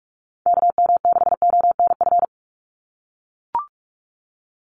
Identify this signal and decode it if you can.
Morse code, heard as a keyed tone, about 44 wpm, 700 Hz. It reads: KM6ONF
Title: USA Call Signs 45wpm